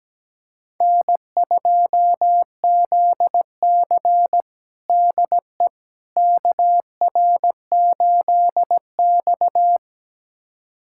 Morse code, heard as a keyed tone, about 17 wpm, 700 Hz. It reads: N2ZC DE KR8X
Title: USA Callsign DE Callsign - 4 Characters - 17wpm